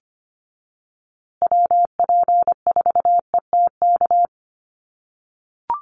Morse code, heard as a keyed tone, about 25 wpm, 700 Hz. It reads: WP4ETK
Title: USA Call Signs Repeated 2x 25wpm